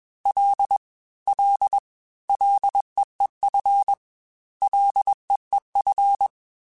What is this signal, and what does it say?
Morse code, heard as a keyed tone, about 21 wpm, 800 Hz. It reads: L L LEEF LEEF
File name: F and L / F and L mp3 / F and L Prefixes 3x2 Twice @20